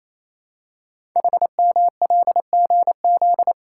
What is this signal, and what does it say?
Morse code, heard as a keyed tone, about 28 wpm, 700 Hz. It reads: HMLGZ
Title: Code Groups - 5 Letters - 28wpm